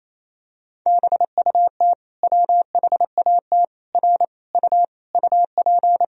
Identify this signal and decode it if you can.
Morse code, heard as a keyed tone, about 28 wpm, 700 Hz. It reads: BUT WHAT R U UP
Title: Sentences from Top 100 Words and Abbreviations 28wpm